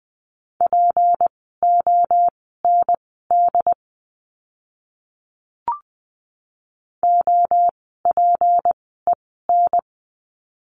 Morse code, heard as a keyed tone, about 20 wpm, spaced as for 14 wpm, 700 Hz. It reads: POND OPEN
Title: 153 Lesson 25f - Letter p - Words 20wpm